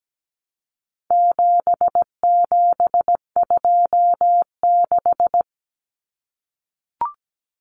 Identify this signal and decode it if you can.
Morse code, heard as a keyed tone, about 17 wpm, 700 Hz. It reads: 7726
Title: Numbers - 4-Digit 17wpm